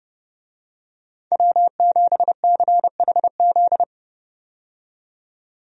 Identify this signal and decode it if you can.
Morse code, heard as a keyed tone, about 30 wpm, 700 Hz. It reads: W7CHZ
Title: USA Call Signs - 5 Characters - 30wpm